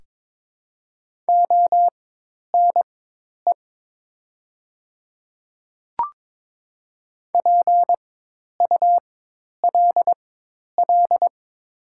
Morse code, heard as a keyed tone, about 22 wpm, spaced as for 10 wpm, 700 Hz. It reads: ONE PULL
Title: 4x Character Spacing - Top 500 Words - 22wpm